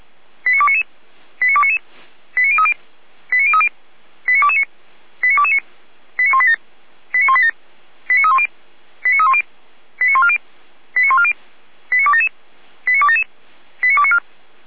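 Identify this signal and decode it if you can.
Five-tone selective call (ZVEI), heard as quick runs of five tones, tones of about 70 ms. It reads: 89299 89299 89030 89030 89229 89229 89187 89187 89310 89310 89140 89140 89277 89277 89294
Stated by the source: ZVEI five-tone selective call received on an SDR.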